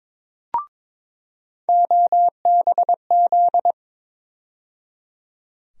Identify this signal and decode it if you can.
Morse code, heard as a keyed tone, about 21 wpm, 700 Hz. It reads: OBZ
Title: USA Call Sign Postfixes - 3 Letters 22wpm